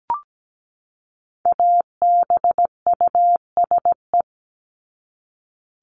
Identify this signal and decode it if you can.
Morse code, heard as a keyed tone, about 17 wpm, 700 Hz. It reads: ABUSE
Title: Top 1251-1500 Words 17wpm